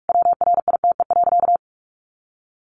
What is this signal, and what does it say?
Morse code, heard as a keyed tone, about 44 wpm, 700 Hz. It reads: WRITE.
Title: Sentences from Top 800 Words 45wpm